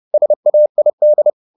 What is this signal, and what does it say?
Morse code, heard as a keyed tone, about 30 wpm, 600 Hz. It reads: SAID